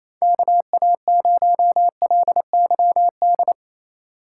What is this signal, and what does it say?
Morse code, heard as a keyed tone, about 28 wpm, 700 Hz. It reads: KA0LYD
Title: USA Call Signs - 6 Characters - 3x Character Spacing - 28wpm